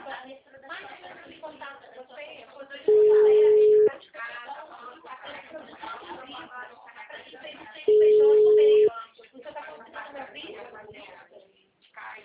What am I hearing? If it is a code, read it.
Ringback tone (Europe, 425 Hz), with speech.